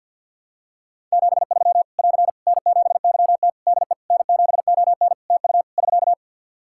Morse code, heard as a keyed tone, about 51 wpm, 700 Hz. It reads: 73 <AR> N7YT DE N7ON TU <SK>
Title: Practice QSO - Exchange 4 - Ending - 50wpm